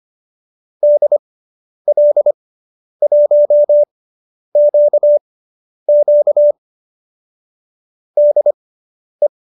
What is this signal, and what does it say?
Morse code, heard as a keyed tone, about 25 wpm, spaced as for 10 wpm, 590 Hz. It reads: DL1QQ DE